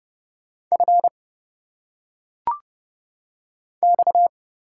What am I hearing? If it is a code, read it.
Morse code, heard as a keyed tone, about 30 wpm, 700 Hz. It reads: F X
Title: Single Letters - Phonetic Pronunciation 30wpm